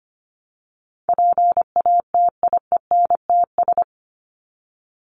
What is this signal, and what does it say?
Morse code, heard as a keyed tone, about 25 wpm, 700 Hz. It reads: PATIENTS
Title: ES:S Suffix - Top 100 Words with Base Words 25wpm